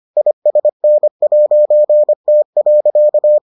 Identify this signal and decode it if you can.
Morse code, heard as a keyed tone, about 25 wpm, 600 Hz. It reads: ISN'T.